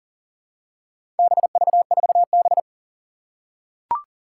Morse code, heard as a keyed tone, about 41 wpm, 700 Hz. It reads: BV4B
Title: Four Challenging Letter-Number Combinations 40wpm